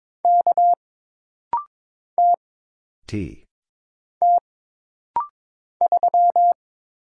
Morse code, heard as a keyed tone, about 22 wpm, 700 Hz. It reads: K T T 3
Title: Single Letter-Number Alt - Rapid-Fire 22wpm